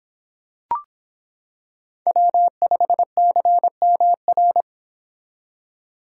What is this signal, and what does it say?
Morse code, heard as a keyed tone, about 25 wpm, 700 Hz. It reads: W5CMR